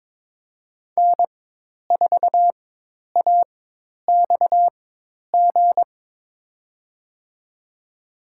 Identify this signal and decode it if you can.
Morse code, heard as a keyed tone, about 21 wpm, spaced as for 10 wpm, 700 Hz. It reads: N4AXG
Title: USA Call Signs - 5 Characters - 4x Character Spacing - 22wpm